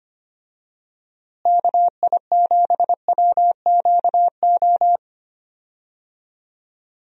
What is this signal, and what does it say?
Morse code, heard as a keyed tone, about 25 wpm, 700 Hz. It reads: KI7WQO
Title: USA Call Signs - 6 Characters - Repeated 3x - 25wpm